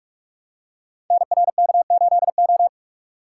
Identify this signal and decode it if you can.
Morse code, heard as a keyed tone, about 44 wpm, 700 Hz. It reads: NRK8O
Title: Code Groups - 5 Letter:Numbers - 45wpm